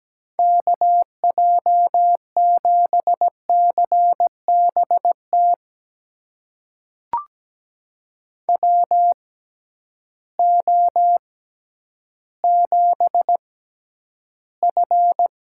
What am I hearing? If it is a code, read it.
Morse code, heard as a keyed tone, about 17 wpm, spaced as for 10 wpm, 700 Hz. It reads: KJ7CBT WO7F
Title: USA Call Signs - 6x Character Spacing - 17wpm